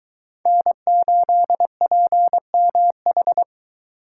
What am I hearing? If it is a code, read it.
Morse code, heard as a keyed tone, about 23 wpm, 700 Hz. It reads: N8PMH